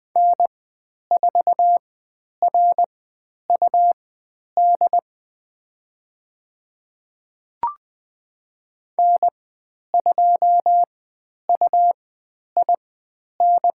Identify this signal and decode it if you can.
Morse code, heard as a keyed tone, about 20 wpm, spaced as for 10 wpm, 700 Hz. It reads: N4RUD N2UIN
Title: 109 Lesson 18g - Letter m - Call Signs 20wpm